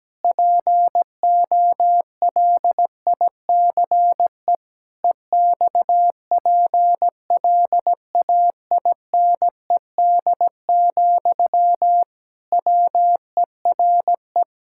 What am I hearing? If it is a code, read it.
Morse code, heard as a keyed tone, about 17 wpm, 700 Hz. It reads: POLICE EXPLAINED, WERE
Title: Pangram Challenge v2 - 17wpm